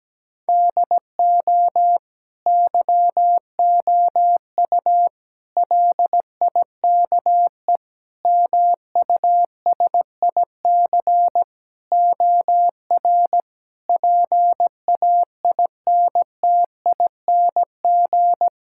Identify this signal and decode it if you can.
Morse code, heard as a keyed tone, about 17 wpm, 700 Hz. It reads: DO YOU LIKE MUSIC OR PAINTING
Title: Sentences from Top 1000 Words - Review Part 6 - 17wpm